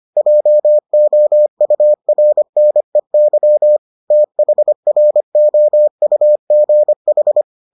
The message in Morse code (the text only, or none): JOURNEY THROUGH